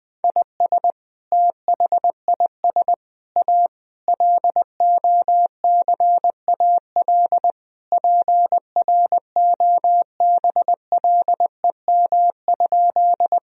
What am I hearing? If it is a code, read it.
Morse code, heard as a keyed tone, about 20 wpm, 700 Hz. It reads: IS THIS A LOCAL PROBLEM?
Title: Sentences from Top 1000 Words - Review Part 3 - 20wpm